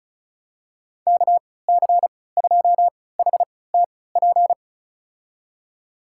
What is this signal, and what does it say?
Morse code, heard as a keyed tone, about 36 wpm, spaced as for 20 wpm, 700 Hz. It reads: KC2HTP
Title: USA Call Signs - 6 Characters - 3x Character Spacing - 35wpm